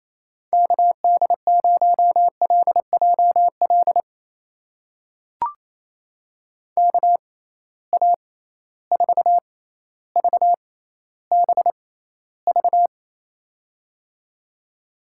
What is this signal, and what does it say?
Morse code, heard as a keyed tone, about 27 wpm, spaced as for 14 wpm, 700 Hz. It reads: KD0LJL KA4VBV
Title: USA Call Signs - 6 Characters - 6x Character Spacing - 28wpm